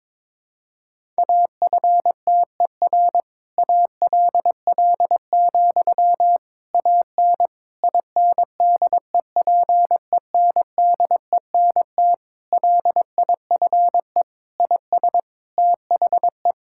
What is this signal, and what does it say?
Morse code, heard as a keyed tone, about 22 wpm, 700 Hz. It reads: AFTER ALL, AN INDEPENDENT LIFE IS THE
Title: Sentences from Top 1000 Words - Review Part 6 - 22wpm